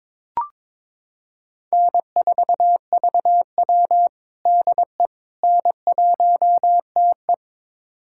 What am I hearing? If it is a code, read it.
Morse code, heard as a keyed tone, about 22 wpm, 700 Hz. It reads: N4VW DE N1TE
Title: USA Callsign DE Callsign - 4 Characters - 22wpm